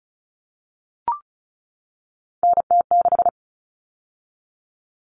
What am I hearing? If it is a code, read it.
Morse code, heard as a keyed tone, about 34 wpm, 700 Hz. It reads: NT6
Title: All Cut Numbers - 3-Digit 35wpm